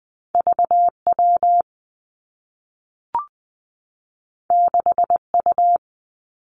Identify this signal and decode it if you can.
Morse code, heard as a keyed tone, about 20 wpm, 700 Hz. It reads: V W 6 U
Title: Two Challenging Letter-Number Combinations 20wpm